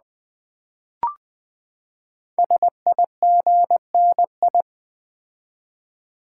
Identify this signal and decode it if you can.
Morse code, heard as a keyed tone, about 20 wpm, 700 Hz. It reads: SIGNI
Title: Build A Word - 5 Letters At A Time - 12-Letter Words - 20wpm